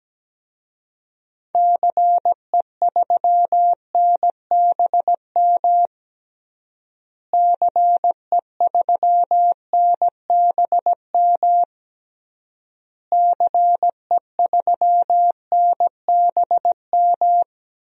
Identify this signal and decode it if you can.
Morse code, heard as a keyed tone, about 17 wpm, 700 Hz. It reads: CE3NBM CE3NBM CE3NBM
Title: DXCC-100 Call Signs - 6 Characters - Repeated 3x - 17wpm